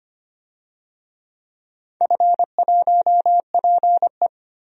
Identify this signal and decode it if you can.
Morse code, heard as a keyed tone, about 25 wpm, 700 Hz. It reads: F1PE
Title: DXCC-100 Call Signs - 4 Characters - 25wpm